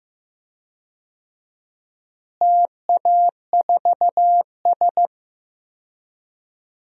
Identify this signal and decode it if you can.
Morse code, heard as a keyed tone, about 15 wpm, 700 Hz. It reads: TA4S
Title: DXCC-100 Call Signs - 4 Characters - Repeated 3x - 15wpm